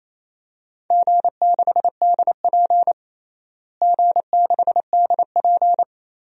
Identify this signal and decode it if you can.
Morse code, heard as a keyed tone, about 28 wpm, 700 Hz. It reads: G6DP G6DP
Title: DXCC-100 Call Signs - 4 Characters - Repeated 2x - 28wpm